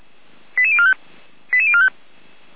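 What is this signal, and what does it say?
Five-tone selective call (ZVEI), heard as quick runs of five tones, tones of about 70 ms. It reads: 88045 88045
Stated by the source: ZVEI five-tone call 88265 received by SDR, on an audio server of the Feuerwehr Orlamünde fire brigade.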